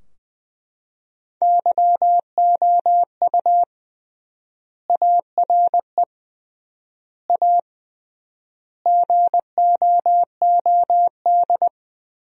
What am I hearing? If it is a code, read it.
Morse code, heard as a keyed tone, about 20 wpm, 700 Hz. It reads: YOU ARE A GOOD
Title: 3x Word-Spacing - Sentences from Top 200 Words - 20wpm